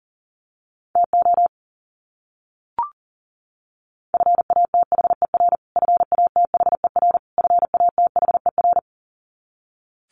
Morse code, heard as a keyed tone, about 40 wpm, 700 Hz. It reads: TO FATHER FATHER FATHER
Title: Top 300 Words Repeated 40wpm